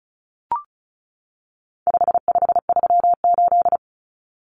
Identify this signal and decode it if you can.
Morse code, heard as a keyed tone, about 34 wpm, 700 Hz. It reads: 5538
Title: Common Cut Numbers - 4-Digit 35wpm